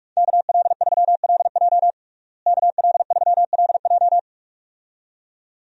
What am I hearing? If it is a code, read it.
Morse code, heard as a keyed tone, about 46 wpm, 700 Hz. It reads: KL3LJ KL3LJ
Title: USA Call Signs - 5 Characters - Repeated 2x - 45wpm